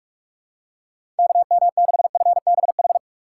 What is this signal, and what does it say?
Morse code, heard as a keyed tone, about 43 wpm, 700 Hz. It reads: KM6FBH
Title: USA Call Signs - 6 Characters - 4x Character Spacing - 45wpm